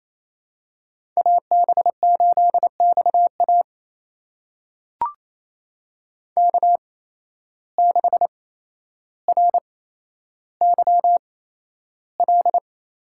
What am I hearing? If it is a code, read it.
Morse code, heard as a keyed tone, about 28 wpm, spaced as for 12 wpm, 700 Hz. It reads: AB8XA K6RYL
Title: USA Call Signs - 5 Characters - 8x Character Spacing - 28wpm